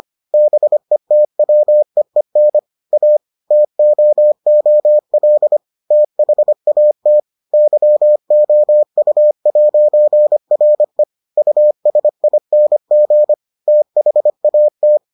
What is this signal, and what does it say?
Morse code, heard as a keyed tone, about 25 wpm, 600 Hz. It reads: BETWEEN A TOOL THAT YOU'RE USING THAT